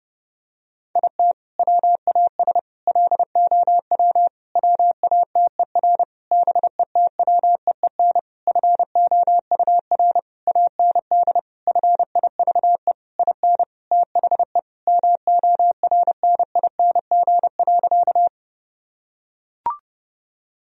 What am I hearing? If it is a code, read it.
Morse code, heard as a keyed tone, about 30 wpm, 700 Hz. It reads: IT WAS LOW WATER BETWEEN FOUR AND FIVE IN THE MORNING.
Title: Sentences from Top 1000 Words - Review Part 2 - 30wpm